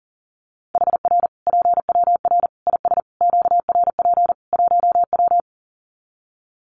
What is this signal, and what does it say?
Morse code, heard as a keyed tone, about 40 wpm, 700 Hz. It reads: HR PWR IS QRP 1W
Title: Common QSO Phrases 40wpm